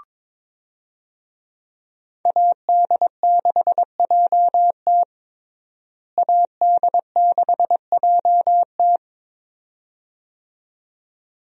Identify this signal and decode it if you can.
Morse code, heard as a keyed tone, about 22 wpm, 700 Hz. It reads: AD6JT AD6JT
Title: USA Call Signs - 5 Characters - Repeated 2x - 22wpm